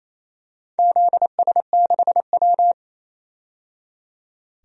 Morse code, heard as a keyed tone, about 28 wpm, 700 Hz. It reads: ZS6W
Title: DXCC-100 Call Signs - 4 Characters - 28wpm